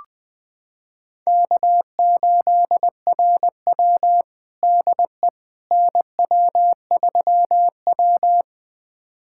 Morse code, heard as a keyed tone, about 20 wpm, 700 Hz. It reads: K8RW DE NW3W
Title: USA Callsign DE Callsign - 4 Characters - 20wpm